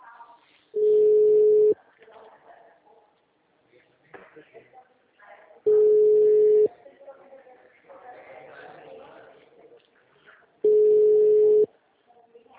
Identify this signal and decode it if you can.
Ringback tone (Europe, 425 Hz), with speech.
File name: out- -2010- - - .68